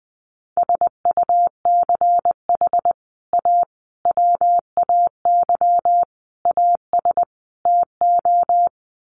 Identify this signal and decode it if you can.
Morse code, heard as a keyed tone, about 20 wpm, 700 Hz. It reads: SUCH A WAY AS TO
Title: Sets of 5 Words Encore2 20wpm